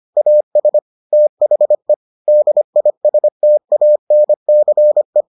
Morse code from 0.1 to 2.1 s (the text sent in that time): AS THE